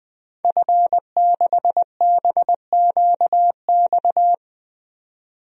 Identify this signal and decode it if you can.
Morse code, heard as a keyed tone, about 20 wpm, 700 Hz. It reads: F6BQX
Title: DXCC-100 Call Signs - 5 Characters - 20wpm